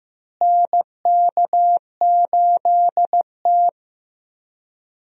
Morse code, heard as a keyed tone, about 15 wpm, 700 Hz. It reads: NK8T
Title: USA Call Signs - 4 Characters - Repeated 3x - 15wpm